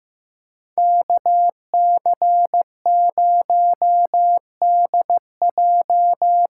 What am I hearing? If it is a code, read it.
Morse code, heard as a keyed tone, about 15 wpm, 700 Hz. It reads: KC0DJ
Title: USA Call Signs - 6 Characters - 6x Character Spacing - 15wpm